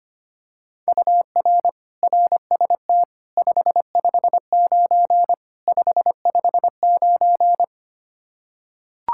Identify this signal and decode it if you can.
Morse code, heard as a keyed tone, about 25 wpm, 700 Hz. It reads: UR RST 559 559
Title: Practice QSO - Exchange 1 - Beginning - 25wpm